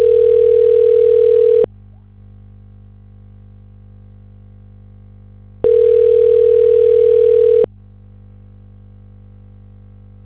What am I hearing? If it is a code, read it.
Ringback tone (North America), 440 + 480 Hz.